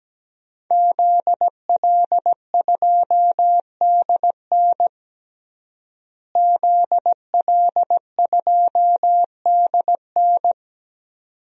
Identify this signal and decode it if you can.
Morse code, heard as a keyed tone, about 17 wpm, 700 Hz. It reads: ZL2DN ZL2DN
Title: DXCC-100 Call Signs - 5 Characters - Repeated 2x - 17wpm